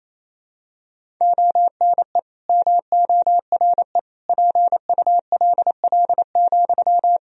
Morse code, heard as a keyed tone, about 28 wpm, 700 Hz. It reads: ONE MORE PULL,
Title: Sentences from Top 1000 Words - Review Part 2 - 28wpm